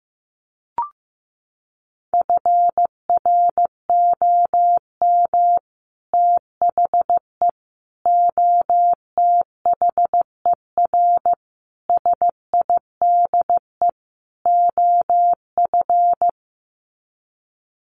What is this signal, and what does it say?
Morse code, heard as a keyed tone, about 15 wpm, 700 Hz. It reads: FROM THE OTHER SIDE OF
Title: Sets of 5 Words Encore 15wpm